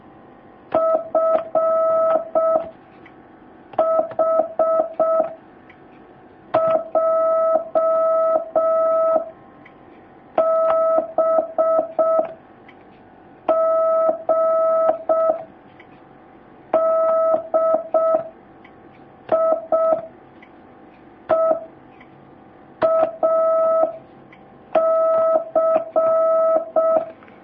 Morse code, heard as a keyed tone, about 6 wpm, 1300 Hz. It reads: F H J B G D I E A C
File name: jh8cbh / 2019cw